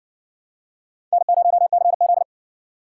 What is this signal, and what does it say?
Morse code, heard as a keyed tone, about 59 wpm, 700 Hz. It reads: N0CZ